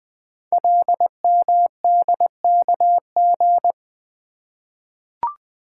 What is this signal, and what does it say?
Morse code, heard as a keyed tone, about 20 wpm, 700 Hz. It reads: LMDKG
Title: Code Groups - 5 Letters - 20wpm